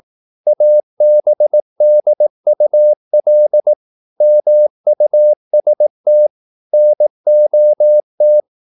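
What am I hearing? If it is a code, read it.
Morse code, heard as a keyed tone, about 18 wpm, 600 Hz. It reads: ABDUL MUST NOT
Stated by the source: Morse practice at 18 wpm, English text, from CW med Gnister.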